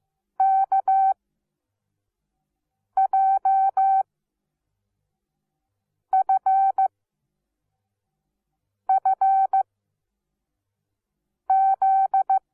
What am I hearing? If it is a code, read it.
Morse code, heard as a keyed tone, about 15 wpm, spaced as for 4 wpm, 770 Hz. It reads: KJFFZ
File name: Onlinelehrgang / morsen / DJ4UF Morsen17